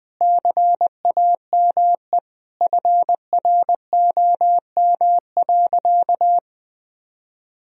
Morse code, heard as a keyed tone, about 20 wpm, 700 Hz. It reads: CAME FROM.
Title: Sentences from Top 1000 Words - Review Part 4 - 20wpm